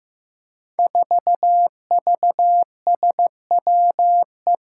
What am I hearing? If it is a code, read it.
Morse code, heard as a keyed tone, about 15 wpm, 700 Hz. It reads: 4VSWE
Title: Five Challenging Letter-Number Combinations 15wpm